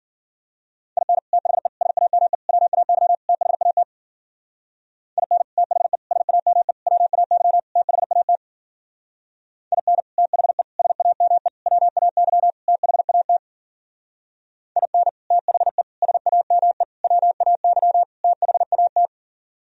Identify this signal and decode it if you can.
Morse code, heard as a keyed tone, about 47 wpm, 700 Hz. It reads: IN THE SAME WAY THAT IN THE SAME WAY THAT IN THE SAME WAY THAT IN THE SAME WAY THAT
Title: Speed-Racing - Sets of 5 Words 40wpm